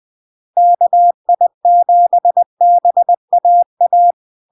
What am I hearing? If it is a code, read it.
Morse code, heard as a keyed tone, about 20 wpm, 700 Hz. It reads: KI7BAA
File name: LARK-FD-2022-Log-in-CW-20wpm-700-Hz